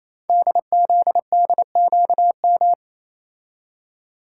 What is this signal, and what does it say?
Morse code, heard as a keyed tone, about 28 wpm, 700 Hz. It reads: DZDQM
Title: Code Groups - 5 Letters - 28wpm